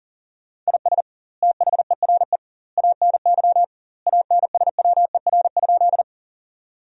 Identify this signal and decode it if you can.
Morse code, heard as a keyed tone, about 40 wpm, 700 Hz. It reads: IS THERE ANY ANSWER?
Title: 2x Word-Spacing - Sentences from Top 300 Words - 40wpm